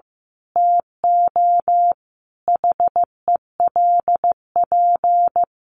Morse code, heard as a keyed tone, about 15 wpm, 700 Hz. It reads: TO HELP
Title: Sentences from Top 1200 Words 15wpm